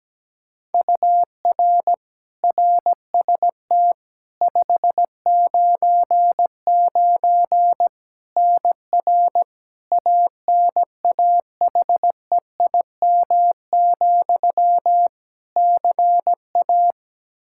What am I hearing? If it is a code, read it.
Morse code, heard as a keyed tone, about 17 wpm, 700 Hz. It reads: UR RST 599 NR ANAHEIM, CA
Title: Practice QSO - Exchange 1 - Beginning - 17wpm